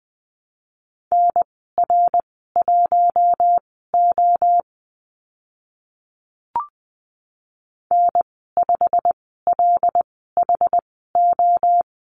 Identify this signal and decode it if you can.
Morse code, heard as a keyed tone, about 20 wpm, spaced as for 14 wpm, 700 Hz. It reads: NR1O N5LHO
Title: 154 Lesson 25g - Letter p - Call Signs 20wpm